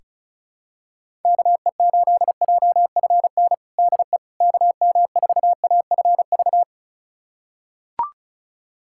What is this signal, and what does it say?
Morse code, heard as a keyed tone, about 34 wpm, 690 Hz. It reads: KE8JFN DE KM4AFV
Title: USA Callsign DE Callsign - 6 Characters - 35wpm